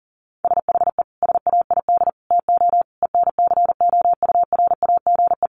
Morse code, heard as a keyed tone, about 40 wpm, 700 Hz. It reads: SHE SAID TO ENCOURAGE
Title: Sentences from Top 1400 Words 40wpm